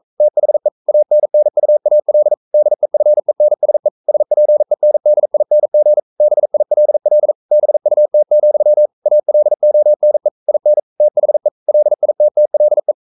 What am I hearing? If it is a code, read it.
Morse code, heard as a keyed tone, about 42 wpm, 600 Hz. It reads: THE ANNUAL DEFENSE SPENDING BILL BUT, ALONE IN THE LITTLE
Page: Romanen Never af Ken Follet på engelsk.